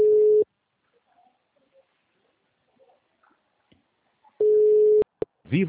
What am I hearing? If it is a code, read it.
Ringback tone (Europe, 425 Hz), with speech.
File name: out- -2021- - - .936